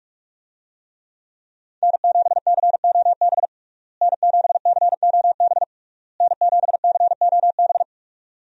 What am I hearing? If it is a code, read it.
Morse code, heard as a keyed tone, about 46 wpm, 700 Hz. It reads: N7COB N7COB N7COB
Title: USA Call Signs - 5 Characters - Repeated 3x - 45wpm